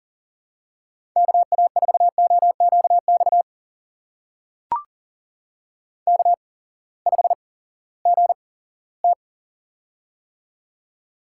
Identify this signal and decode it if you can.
Morse code, heard as a keyed tone, about 42 wpm, spaced as for 20 wpm, 700 Hz. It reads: KA4OQX K5GT
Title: USA Call Signs - 8x Character Spacing - 40wpm